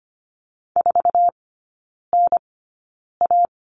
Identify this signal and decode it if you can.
Morse code, heard as a keyed tone, about 25 wpm, spaced as for 9 wpm, 700 Hz. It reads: 4NA
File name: Lesson 17e - Letter u - Three Characters 25wpm